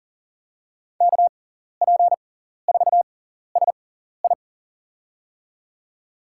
Morse code, heard as a keyed tone, about 40 wpm, spaced as for 14 wpm, 700 Hz. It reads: KP4SI
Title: USA Call Signs - 5 Characters - 6x Character Spacing - 40wpm